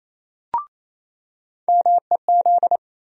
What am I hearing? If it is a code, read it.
Morse code, heard as a keyed tone, about 28 wpm, 700 Hz. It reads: MEZ
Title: USA Call Sign Postfixes - 3 Letters 28wpm